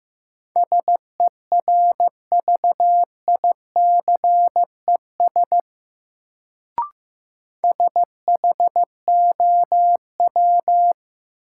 Morse code, heard as a keyed tone, about 15 wpm, 700 Hz. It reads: SERVICES SHOW
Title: Sets of 5 Words - New Word Drill - 15wpm